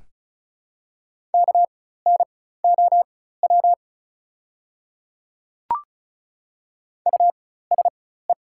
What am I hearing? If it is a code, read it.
Morse code, heard as a keyed tone, about 36 wpm, spaced as for 16 wpm, 690 Hz. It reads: KNOW USE
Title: 4x Character Spacing - Top 100 Words - 35wpm